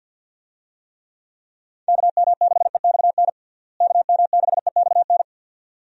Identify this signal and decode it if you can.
Morse code, heard as a keyed tone, about 51 wpm, 700 Hz. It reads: KM6EXN KM6EXN
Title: USA Call Signs - 6 Characters - Repeated 2x - 50wpm